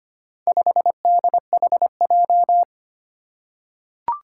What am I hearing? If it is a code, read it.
Morse code, heard as a keyed tone, about 25 wpm, 700 Hz. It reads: 5DHJ
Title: Four Challenging Letter-Number Combinations 25wpm